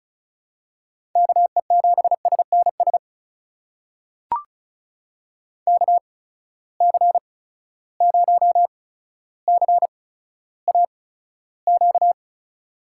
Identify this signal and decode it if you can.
Morse code, heard as a keyed tone, about 34 wpm, spaced as for 15 wpm, 700 Hz. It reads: KE7SNS KC0CAQ
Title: USA Call Signs - 6 Characters - 8x Character Spacing - 35wpm